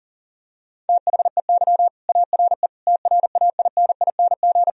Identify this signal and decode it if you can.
Morse code, heard as a keyed tone, about 40 wpm, 700 Hz. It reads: THEY ARE TRAINING
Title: Sentences from Top 1000 Words - Review Part 4 - 40wpm